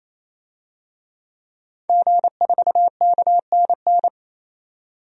Morse code, heard as a keyed tone, about 28 wpm, 700 Hz. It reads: G4KNN
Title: DXCC-100 Call Signs - 5 Characters - 28wpm